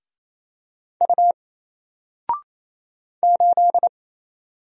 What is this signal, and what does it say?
Morse code, heard as a keyed tone, about 27 wpm, 700 Hz. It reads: U 8
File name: Single-Letter-Number-28wpm